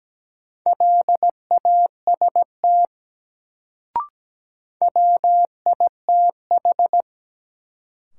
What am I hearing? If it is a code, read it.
Morse code, heard as a keyed tone, about 17 wpm, 700 Hz. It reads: LAST WITH
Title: Sets of 2 Words - New Word Drill - 17wpm